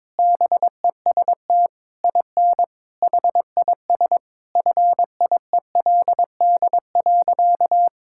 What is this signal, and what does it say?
Morse code, heard as a keyed tone, about 22 wpm, 700 Hz. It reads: BEST IN HIS FIELD.
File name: Sentences with 7-Word Phrases - 22wpm